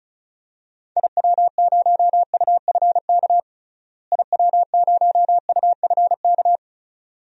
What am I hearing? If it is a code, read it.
Morse code, heard as a keyed tone, about 36 wpm, 700 Hz. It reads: IW0UFK IW0UFK
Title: DXCC-100 Call Signs - 6 Characters - Repeated 2x - 35wpm